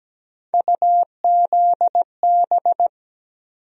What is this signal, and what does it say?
Morse code, heard as a keyed tone, about 17 wpm, 700 Hz. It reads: UZB
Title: USA Call Sign Postfixes - 3 Letters 17wpm